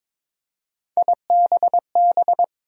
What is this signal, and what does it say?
Morse code, heard as a keyed tone, about 22 wpm, 700 Hz. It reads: IBB
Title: USA Call Sign Postfixes - 3 Letters 22wpm